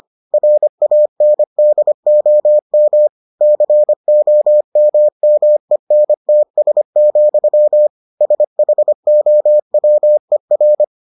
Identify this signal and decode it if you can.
Morse code, heard as a keyed tone, about 25 wpm, 600 Hz. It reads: RANDOM COMMENTS, SHOWER